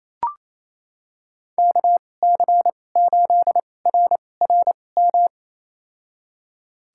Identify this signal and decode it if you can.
Morse code, heard as a keyed tone, about 28 wpm, spaced as for 20 wpm, 700 Hz. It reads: KC8RRM
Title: USA Call Signs - 2x Character Spacing - 28wpm